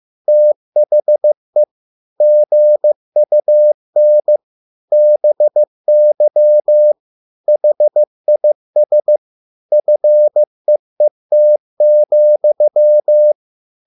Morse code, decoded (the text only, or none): THE GUN BY HIS FEET,